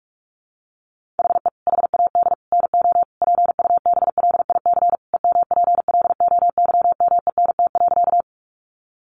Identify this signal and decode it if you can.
Morse code, heard as a keyed tone, about 44 wpm, 700 Hz. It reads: HE HAD NO PUBLIC EMPLOYMENT.
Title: Sentences from Top 1500 Words 45wpm